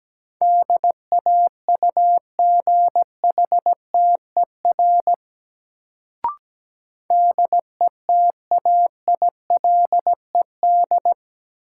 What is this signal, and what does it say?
Morse code, heard as a keyed tone, about 17 wpm, 700 Hz. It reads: DAUGHTER DETAILED
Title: Sentences from Top 1500 Words - New Word Drill - 17wpm